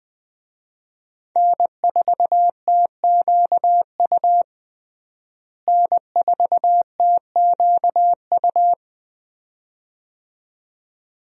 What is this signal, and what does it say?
Morse code, heard as a keyed tone, about 20 wpm, 700 Hz. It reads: N4TQU N4TQU
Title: USA Call Signs - 5 Characters - Repeated 2x - 20wpm